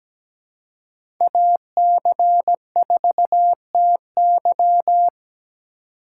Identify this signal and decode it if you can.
Morse code, heard as a keyed tone, about 17 wpm, 700 Hz. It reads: AC4TY
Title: USA Call Signs - 5 Characters - 2x Character Spacing - 17wpm